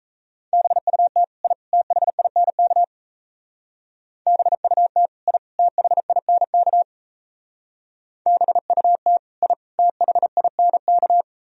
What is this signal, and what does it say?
Morse code, heard as a keyed tone, about 34 wpm, 700 Hz. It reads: BUT I THINK BUT I THINK BUT I THINK
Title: Speed-Racing - Sets of 3 Words 28wpm